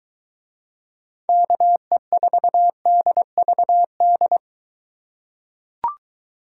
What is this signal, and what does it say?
Morse code, heard as a keyed tone, about 23 wpm, 700 Hz. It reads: KE4DVD